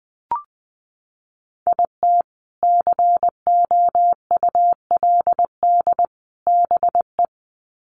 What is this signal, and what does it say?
Morse code, heard as a keyed tone, about 20 wpm, 700 Hz. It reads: IT COULD BE
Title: Sets of 3 Words 20wpm